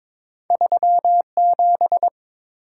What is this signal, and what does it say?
Morse code, heard as a keyed tone, about 32 wpm, spaced as for 27 wpm, 700 Hz. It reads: 37
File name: Speed-Racing - Two Letter-Number Combinations - 22wpm